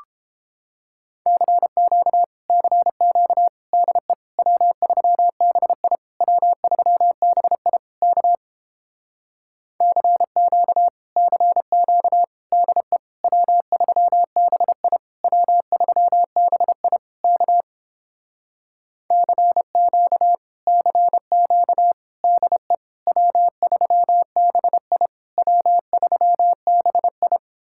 Morse code, heard as a keyed tone, about 29 wpm, 700 Hz. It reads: CQ CQ DE W3BI W3BI K CQ CQ DE W3BI W3BI K CQ CQ DE W3BI W3BI